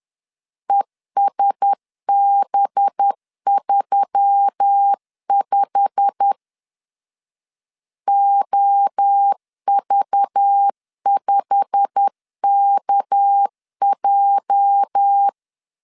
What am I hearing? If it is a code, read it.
Morse code, heard as a keyed tone, about 10 wpm, 800 Hz. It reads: ESB35 OV5KJ
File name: LEZIONI / 8 YJ45 / 8 YJ45 10wpm 2549